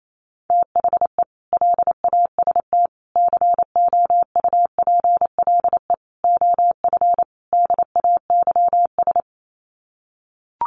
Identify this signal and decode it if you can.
Morse code, heard as a keyed tone, about 28 wpm, 700 Hz. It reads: THE LAST COUPLE OF DAYS
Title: Sets of 5 Words Encore 28wpm